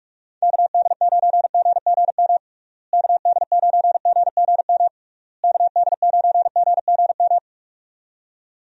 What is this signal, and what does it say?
Morse code, heard as a keyed tone, about 44 wpm, 700 Hz. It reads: KD9GGM KD9GGM KD9GGM
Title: USA Call Signs - 6 Characters - Repeated 3x - 45wpm